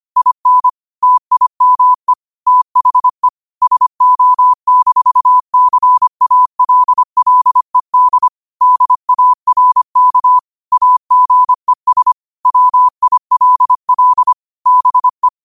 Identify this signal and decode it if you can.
Morse code, heard as a keyed tone, about 25 wpm, 1000 Hz. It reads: IN TIME THE SO-CALLED DARK AGES WILL BE
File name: qotdfiles / 25wpm